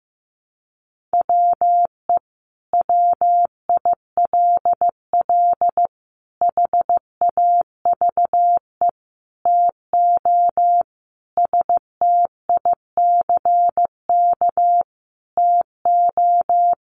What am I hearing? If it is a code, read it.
Morse code, heard as a keyed tone, about 15 wpm, 700 Hz. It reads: WE WILL HAVE TO STICK TO
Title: Sentences from Top 1200 Words 15wpm